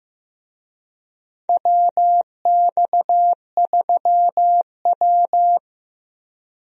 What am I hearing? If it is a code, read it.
Morse code, heard as a keyed tone, about 15 wpm, 700 Hz. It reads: WX3W
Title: USA Call Signs - 4 Characters - 15wpm